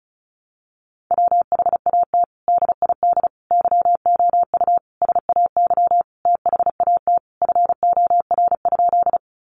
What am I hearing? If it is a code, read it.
Morse code, heard as a keyed tone, about 34 wpm, 700 Hz. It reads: WHAT DID YOU SAY THAT FOR?
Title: Sentences from Top 200 Words 35wpm